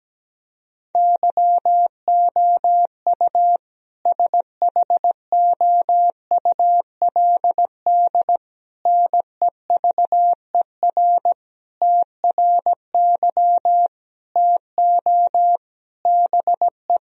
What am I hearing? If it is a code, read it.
Morse code, heard as a keyed tone, about 17 wpm, 700 Hz. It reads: YOU SHOULD NEVER TRY TO BE
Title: Sentences from Top 1000 Words - Review Part 3 - 17wpm